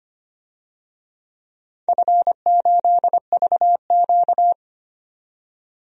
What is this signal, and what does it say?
Morse code, heard as a keyed tone, about 25 wpm, 700 Hz. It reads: F8VQ
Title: DXCC-100 Call Signs - 4 Characters - 25wpm